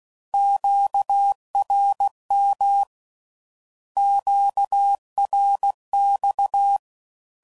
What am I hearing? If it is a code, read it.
Morse code, heard as a keyed tone, about 16 wpm, 800 Hz. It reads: QRM QRX
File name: CW Code Groups @15WPM / Q Calls / Q Calls Twice